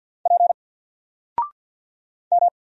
Morse code, heard as a keyed tone, about 48 wpm, 700 Hz. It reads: P M
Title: Single Letter-Number Alt 50wpm